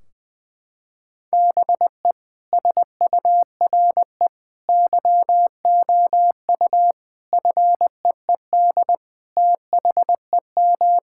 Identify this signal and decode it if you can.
Morse code, heard as a keyed tone, about 20 wpm, 700 Hz. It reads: BE SURE YOU FEED THEM
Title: Sentences from Top 1600 Words 20wpm